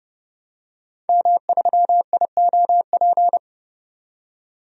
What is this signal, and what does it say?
Morse code, heard as a keyed tone, about 30 wpm, 700 Hz. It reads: M3IOP
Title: DXCC-100 Call Signs - 5 Characters - 30wpm